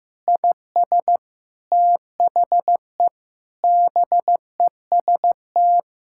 Morse code, heard as a keyed tone, about 15 wpm, 700 Hz. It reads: IS THE BEST
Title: Sentences from Top 1000 Words - Review Part 6 - 15wpm